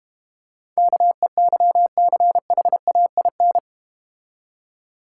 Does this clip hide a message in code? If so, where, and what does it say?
Yes: Morse code from 0.8 to 3.6 s, heard as a keyed tone, about 32 wpm, 700 Hz. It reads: KEYCHAIN